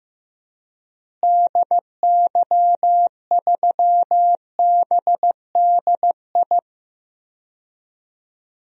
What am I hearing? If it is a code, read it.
Morse code, heard as a keyed tone, about 16 wpm, 700 Hz. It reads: DY3BDI
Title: DXCC-100 Call Signs - 6 Characters - Repeated 3x - 15wpm